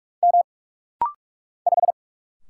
Morse code, heard as a keyed tone, about 44 wpm, 700 Hz. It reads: M 5
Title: Single Letter-Number Alt - Rapid-Fire 45wpm